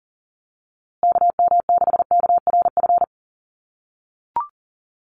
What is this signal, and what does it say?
Morse code, heard as a keyed tone, about 40 wpm, 700 Hz. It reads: KM6KRF
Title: USA Call Signs Repeated 3x 40wpm